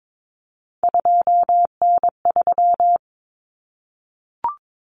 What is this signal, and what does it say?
Morse code, heard as a keyed tone, about 22 wpm, 700 Hz. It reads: 2N3
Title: Common Cut Numbers - 3-Digit 22wpm